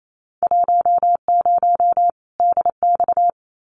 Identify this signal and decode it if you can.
Morse code, heard as a keyed tone, about 28 wpm, 700 Hz. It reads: 10 DX
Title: Winter Field Day Exchanges 28wpm